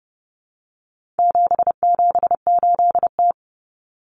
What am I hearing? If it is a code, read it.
Morse code, heard as a keyed tone, about 30 wpm, 700 Hz. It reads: 778T
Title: Common Cut Numbers - 4-Digit 30wpm